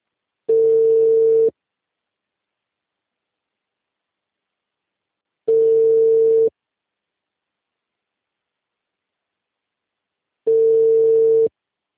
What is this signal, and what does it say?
Ringback tone (China, 450 Hz).